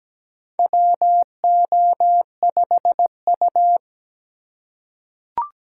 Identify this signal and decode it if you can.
Morse code, heard as a keyed tone, about 17 wpm, 700 Hz. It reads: WO5U
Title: USA Call Signs - 4 Characters - 2x Character Spacing - 17wpm